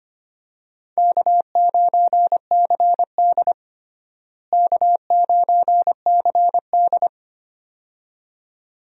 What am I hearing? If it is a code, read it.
Morse code, heard as a keyed tone, about 25 wpm, 700 Hz. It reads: K9CD K9CD
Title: USA Call Signs - 4 Characters - Repeated 2x - 25wpm